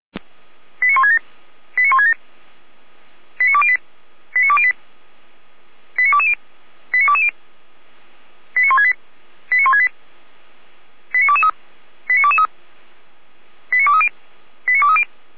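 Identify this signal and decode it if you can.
Five-tone selective call (ZVEI), heard as quick runs of five tones, tones of about 70 ms. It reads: 89167 89167 89208 89208 89220 89220 89168 89168 89223 89223 89230 89230
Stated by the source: ZVEI five-tone selective call 89163, received by SDR.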